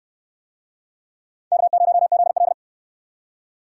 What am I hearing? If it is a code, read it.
Morse code, heard as a keyed tone, about 69 wpm, 700 Hz. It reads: K0ZL